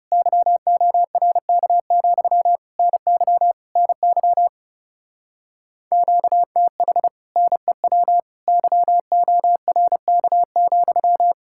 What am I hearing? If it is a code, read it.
Morse code, heard as a keyed tone, about 36 wpm, 700 Hz. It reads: YORK, NY NY QTH NEW YORK,
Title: Speed-Racing Common QSO Phrases - 30wpm